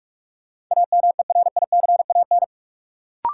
Morse code, heard as a keyed tone, about 41 wpm, 700 Hz. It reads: AMERICAN
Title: Sets of 2 Words - New Word Drill - 45wpm